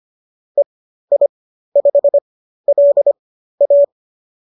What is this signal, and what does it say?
Morse code, heard as a keyed tone, about 25 wpm, spaced as for 13 wpm, 570 Hz. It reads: EI5LA